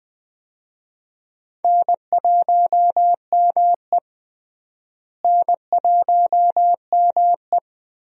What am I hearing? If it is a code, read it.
Morse code, heard as a keyed tone, about 20 wpm, 700 Hz. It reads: N1ME N1ME
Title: USA Call Signs - 4 Characters - Repeated 3x - 20wpm